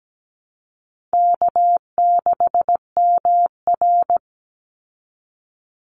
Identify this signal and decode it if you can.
Morse code, heard as a keyed tone, about 17 wpm, 700 Hz. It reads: K6MR
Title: Easy Call Signs Repeated 2x - 17wpm